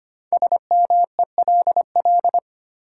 Morse code, heard as a keyed tone, about 25 wpm, 700 Hz. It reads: SMELL
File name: 5 Letter Words - First 3 Letters Given - 25wpm